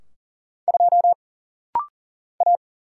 Morse code, heard as a keyed tone, about 40 wpm, 700 Hz. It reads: 2 A
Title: Single Letter-Number - Rapid-Fire 40wpm